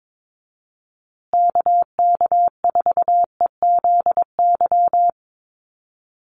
Morse code, heard as a keyed tone, about 22 wpm, 700 Hz. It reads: KK4EZY
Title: USA Call Signs Repeated 2x 22wpm